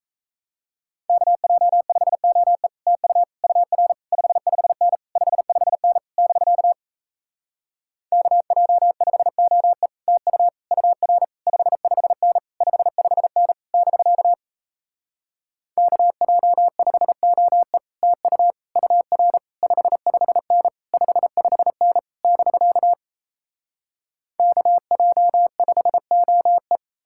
Morse code, heard as a keyed tone, about 37 wpm, 700 Hz. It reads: KJ5OE TU UR 55N 55N <BK> KJ5OE TU UR 55N 55N <BK> KJ5OE TU UR 55N 55N <BK> KJ5OE